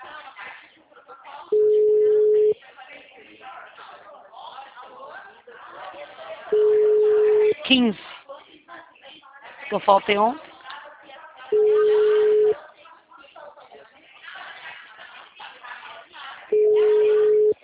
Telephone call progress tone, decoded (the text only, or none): ringback tone